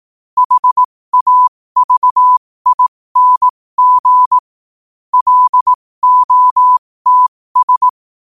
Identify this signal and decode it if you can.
Morse code, heard as a keyed tone, about 18 wpm, spaced as for 15 wpm, 1000 Hz. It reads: HAVING LOTS
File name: qotdfiles / 15wpm